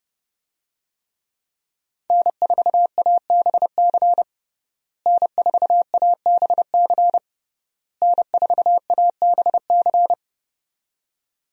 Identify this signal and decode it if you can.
Morse code, heard as a keyed tone, about 30 wpm, 700 Hz. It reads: N4ABC N4ABC N4ABC
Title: USA Call Signs - 5 Characters - Repeated 3x - 30wpm